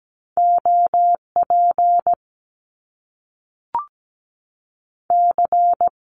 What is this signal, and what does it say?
Morse code, heard as a keyed tone, about 17 wpm, 700 Hz. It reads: OP C
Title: Top Two Letter Combinations in Words 17wpm